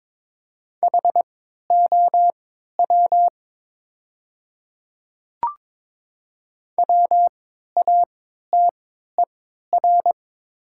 Morse code, heard as a keyed tone, about 22 wpm, spaced as for 13 wpm, 700 Hz. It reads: HOW WATER
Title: 3x Character Spacing - Top 100 Words - 22wpm